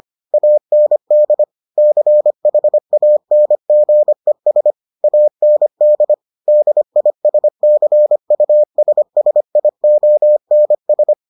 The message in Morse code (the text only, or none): AND CHANGES AND DISCUSSIONS